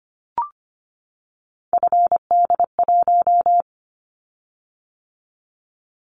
Morse code, heard as a keyed tone, about 25 wpm, 700 Hz. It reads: FD1
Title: Callsign Prefixes for Top 25 DXCC - 25wpm